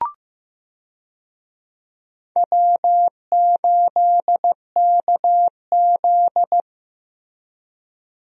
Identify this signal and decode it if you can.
Morse code, heard as a keyed tone, about 15 wpm, 700 Hz. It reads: W8KZ
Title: USA Call Signs - 4 Characters - Repeated 2x - 15wpm